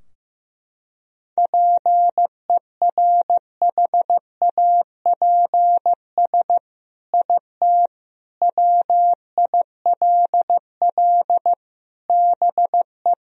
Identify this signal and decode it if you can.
Morse code, heard as a keyed tone, about 15 wpm, 700 Hz. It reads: PERHAPS IT WILL BE
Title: Sentences from Top 1000 Words - Review Part 3 - 15wpm